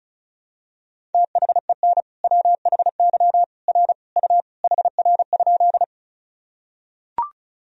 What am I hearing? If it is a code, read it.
Morse code, heard as a keyed tone, about 36 wpm, 700 Hz. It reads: THEN WHY R U HR?
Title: Sentences from Top 200 Words and Abbreviations 35wpm